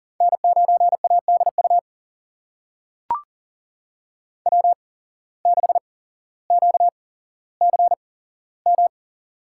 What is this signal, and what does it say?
Morse code, heard as a keyed tone, about 40 wpm, spaced as for 17 wpm, 700 Hz. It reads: N9ADU W6QCM
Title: USA Call Signs - 5 Characters - 8x Character Spacing - 40wpm